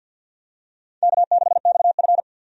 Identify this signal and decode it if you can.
Morse code, heard as a keyed tone, about 52 wpm, 700 Hz. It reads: K6XF
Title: USA Call Signs - 4 Characters - Repeated 2x - 50wpm